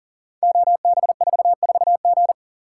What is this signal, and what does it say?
Morse code, heard as a keyed tone, about 38 wpm, 690 Hz. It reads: OB44G
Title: Code Groups - 5 Letter:Numbers - 40wpm